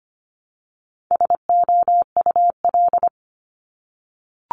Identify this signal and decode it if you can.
Morse code, heard as a keyed tone, about 25 wpm, 700 Hz. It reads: SOUL
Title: Multi-letter Phonemes 25wpm